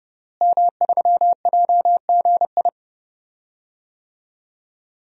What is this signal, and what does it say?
Morse code, heard as a keyed tone, about 30 wpm, 700 Hz. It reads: M3JGI
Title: DXCC-100 Call Signs - 5 Characters - Repeated 2x - 30wpm